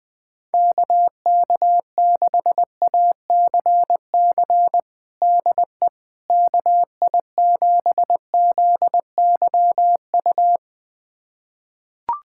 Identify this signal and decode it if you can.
Morse code, heard as a keyed tone, about 20 wpm, 700 Hz. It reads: KK6ACC DE KI7ZYU
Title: USA Callsign DE Callsign - 6 Characters - 20wpm